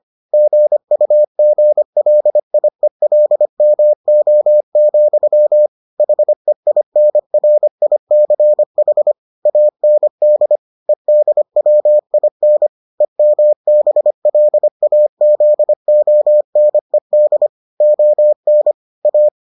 Morse code, heard as a keyed tone, about 25 wpm, 600 Hz. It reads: GUGLIELMO, HEINRICH AND EDWIN EMBLAZONED ON A